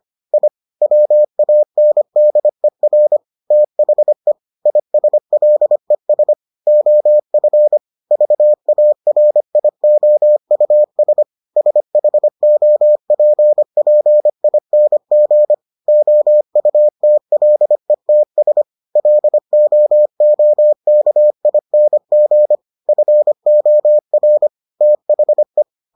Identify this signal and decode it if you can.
Morse code, heard as a keyed tone, about 25 wpm, 600 Hz. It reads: I WANDER THE ISLES OF VARIOUS SHOPPING OUTLETS LOOKING FOR THE